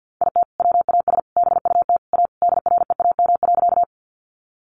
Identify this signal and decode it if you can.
Morse code, heard as a keyed tone, about 50 wpm, 700 Hz. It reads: IT WAS BUT A DREAM.
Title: Sentences from Top 1500 Words 50wpm